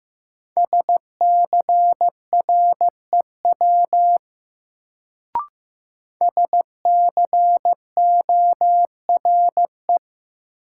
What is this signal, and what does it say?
Morse code, heard as a keyed tone, about 15 wpm, 700 Hz. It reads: SCREW SCORE
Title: Sentences from Top 1600 Words - New Word Drill Alt - 15wpm